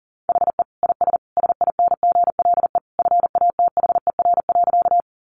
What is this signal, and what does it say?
Morse code, heard as a keyed tone, about 40 wpm, 700 Hz. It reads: HE IS SINGLE FATHER.
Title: Sentences from Top 700 Words 40wpm